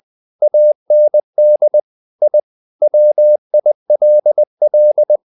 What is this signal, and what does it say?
Morse code, heard as a keyed tone, about 20 wpm, 600 Hz. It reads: AND I WILL
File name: the old man and the sea / the old man and the sea part 1 20wpm